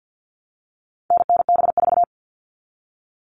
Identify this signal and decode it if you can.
Morse code, heard as a keyed tone, about 50 wpm, 700 Hz. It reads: NNB4
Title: All Cut Numbers - 4-Digit 50wpm